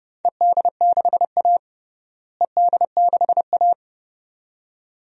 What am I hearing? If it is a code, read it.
Morse code, heard as a keyed tone, about 29 wpm, 700 Hz. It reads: ED6A ED6A
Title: DXCC-100 Call Signs - 4 Characters - Repeated 2x - 30wpm